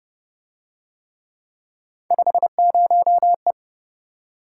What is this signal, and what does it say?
Morse code, heard as a keyed tone, about 28 wpm, 700 Hz. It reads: 50E